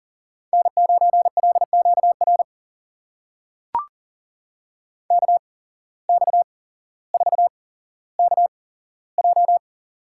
Morse code, heard as a keyed tone, about 40 wpm, spaced as for 17 wpm, 700 Hz. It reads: N9LQR KX4KJ
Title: USA Call Signs - 5 Characters - 8x Character Spacing - 40wpm